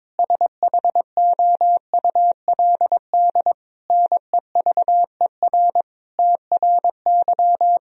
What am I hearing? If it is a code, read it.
Morse code, heard as a keyed tone, about 22 wpm, 700 Hz. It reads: SHOULD NEVER TRY
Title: Sentences from Top 1000 Words - Review Part 3 - 22wpm